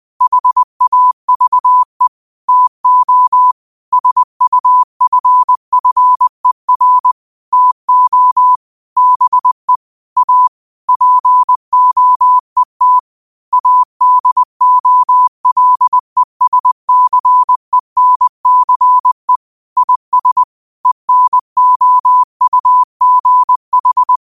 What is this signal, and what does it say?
Morse code, heard as a keyed tone, about 20 wpm, 1000 Hz. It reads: HAVE TO SUFFER TO BE A POET ADOLESCENCE IS ENOUGH
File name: qotdfiles / 20wpm